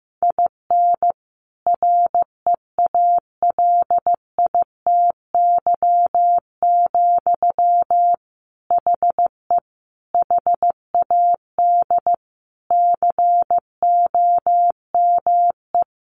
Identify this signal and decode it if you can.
Morse code, heard as a keyed tone, about 15 wpm, 700 Hz. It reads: IN REALITY, HE HAD COME
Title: Sentences from Top 1500 Words 15wpm